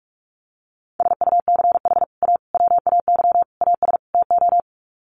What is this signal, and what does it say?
Morse code, heard as a keyed tone, about 44 wpm, 700 Hz. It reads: SUCH A WAY AS TO
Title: Sets of 5 Words Encore2 45wpm